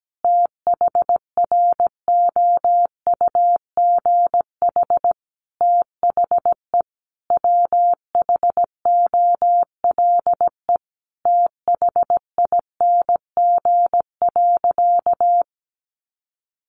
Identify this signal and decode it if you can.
Morse code, heard as a keyed tone, about 17 wpm, 700 Hz. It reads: THROUGH THE WHOLE THING.
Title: Sentences from Top 400 Words 17wpm